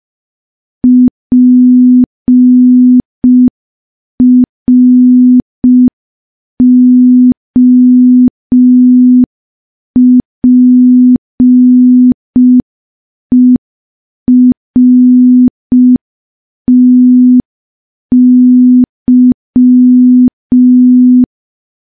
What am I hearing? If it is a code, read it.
Morse code, heard as a keyed tone, about 5 wpm, 250 Hz. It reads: PROPERTY
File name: morse / morse-training-05WPM